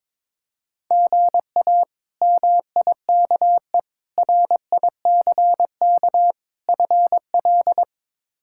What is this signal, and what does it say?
Morse code, heard as a keyed tone, about 22 wpm, 700 Hz. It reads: GA MIKE RICK FL
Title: K1USN SST QSO Practice - 22wpm